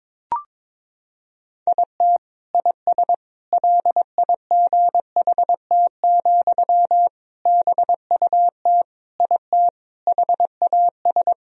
Morse code, heard as a keyed tone, about 21 wpm, 700 Hz. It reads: IT IS LIGHT, BUT IT HAS
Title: Sentences from Top 1000 Words - Review Part 1 - 22wpm